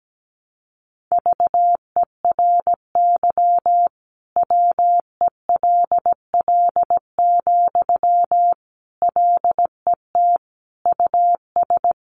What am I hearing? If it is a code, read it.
Morse code, heard as a keyed tone, about 17 wpm, 700 Hz. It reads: VERY WELL, LET US
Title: Sentences from Top 1200 Words 17wpm